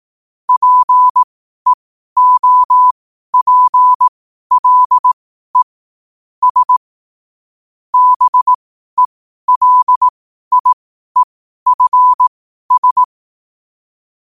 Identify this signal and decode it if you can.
Morse code, heard as a keyed tone, about 18 wpm, spaced as for 12 wpm, 1000 Hz. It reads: PEOPLE S BELIEFS
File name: qotdfiles / 12wpm